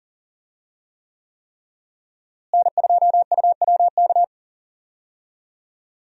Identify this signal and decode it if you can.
Morse code, heard as a keyed tone, about 40 wpm, 700 Hz. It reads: N2UWK
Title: USA Call Signs - 5 Characters - Repeated 3x - 40wpm